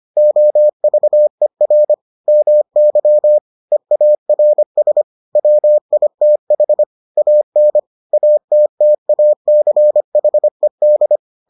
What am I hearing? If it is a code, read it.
Morse code, heard as a keyed tone, about 25 wpm, 600 Hz. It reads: OVER MY EARS WITH AN ATTACHED